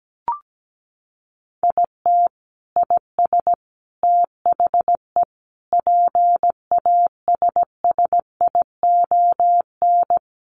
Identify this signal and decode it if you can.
Morse code, heard as a keyed tone, about 17 wpm, 700 Hz. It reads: IT IS THE PASSION
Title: Sentences from Top 1300 Words 17wpm